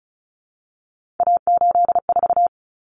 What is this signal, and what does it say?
Morse code, heard as a keyed tone, about 34 wpm, 700 Hz. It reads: A84
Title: Common Cut Numbers - 3-Digit 35wpm